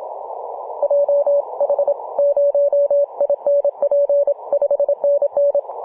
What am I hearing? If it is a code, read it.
Morse code, heard as a keyed tone, about 27 wpm, 570 Hz. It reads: JH0INP 5NN
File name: Mar/26 2219z E51ANQ 18.078MHz CW